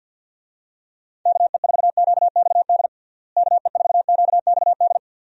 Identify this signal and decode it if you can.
Morse code, heard as a keyed tone, about 51 wpm, 700 Hz. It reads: KE4QXD KE4QXD
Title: USA Call Signs - 6 Characters - Repeated 3x - 50wpm